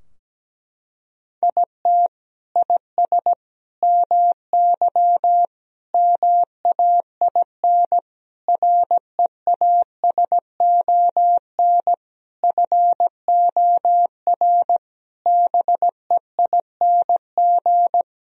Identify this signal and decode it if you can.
Morse code, heard as a keyed tone, about 17 wpm, 700 Hz. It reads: IT IS MY MAIN REASON FOR BEING
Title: Sentences from Top 1000 Words - Review Part 2 - 17wpm